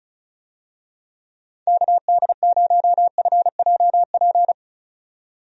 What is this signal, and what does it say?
Morse code, heard as a keyed tone, about 36 wpm, 700 Hz. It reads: KD0FJP
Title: USA Call Signs - 6 Characters - 35wpm